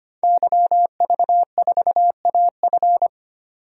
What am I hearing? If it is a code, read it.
Morse code, heard as a keyed tone, about 25 wpm, 700 Hz. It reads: YV4AF
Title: DXCC-100 Call Signs - 5 Characters - 25wpm